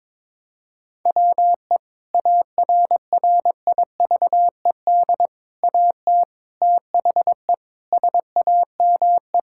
Morse code, heard as a keyed tone, about 22 wpm, 700 Hz. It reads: WE ARRIVED AT THE SAME
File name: practice / 22wpm / Sentences with 4-Word Phrases - 22wpm